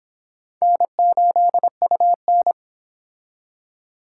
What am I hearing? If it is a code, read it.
Morse code, heard as a keyed tone, about 24 wpm, 700 Hz. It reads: N8UN